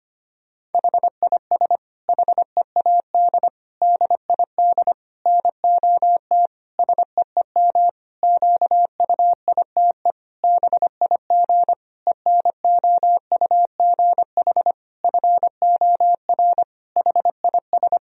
Morse code, heard as a keyed tone, about 25 wpm, 700 Hz. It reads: HIS HEAD DID NOT SEEM QUITE BIG ENOUGH FOR HIS
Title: Sentences from Top 1000 Words - Review Part 2 - 25wpm